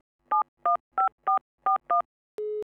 DTMF, heard as telephone touch tones, tones of about 120 ms. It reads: 715441
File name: albums / SFX Technology Vol. 1 [288] / Swedish Telephone Dialling And Engaged [18062] - SFX Producer